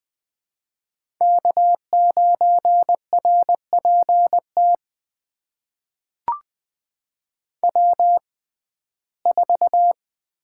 Morse code, heard as a keyed tone, about 21 wpm, spaced as for 15 wpm, 700 Hz. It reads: K9RPT W4
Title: USA Call Signs - 5 Characters - 6x Character Spacing - 20wpm